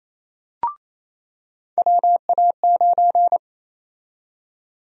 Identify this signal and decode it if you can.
Morse code, heard as a keyed tone, about 28 wpm, 700 Hz. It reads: WA9
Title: USA Call Sign Prefixes - 28wpm